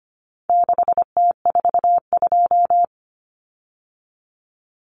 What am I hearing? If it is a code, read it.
Morse code, heard as a keyed tone, about 25 wpm, 700 Hz. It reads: 6T42
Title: Common Cut Numbers - 4-Digit 25wpm